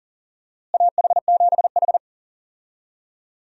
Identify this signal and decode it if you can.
Morse code, heard as a keyed tone, about 40 wpm, 700 Hz. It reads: AH7H
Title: USA Call Signs - 4 Characters - Repeated 3x - 40wpm